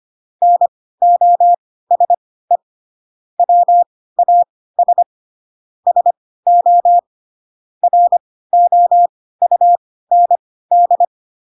Morse code, heard as a keyed tone, about 25 wpm, spaced as for 16 wpm, 700 Hz. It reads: NOSE WAS SO ROUND
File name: Pinocchio / Pinocchio ch1 10 25 ik1hge